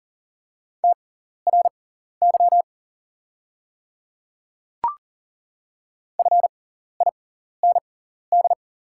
Morse code, heard as a keyed tone, about 40 wpm, spaced as for 14 wpm, 700 Hz. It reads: TRY FIND
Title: 6x Character Spacing - Top 300 Words - 40wpm